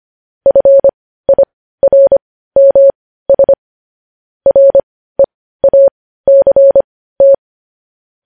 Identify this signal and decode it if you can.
Morse code, heard as a keyed tone, about 25 wpm, spaced as for 15 wpm, 550 Hz. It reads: FIRMS REACT
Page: news headlines in morse code at 15 wpm